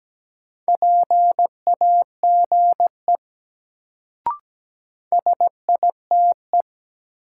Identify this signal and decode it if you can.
Morse code, heard as a keyed tone, about 17 wpm, 700 Hz. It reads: PAGE SITE
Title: Sentences from Top 1000 Words - Review Part 1 - New Word Drill - 17wpm